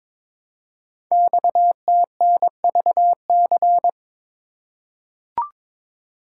Morse code, heard as a keyed tone, about 22 wpm, 700 Hz. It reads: XTNVC
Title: Code Groups - 5 Letter:Numbers - 22wpm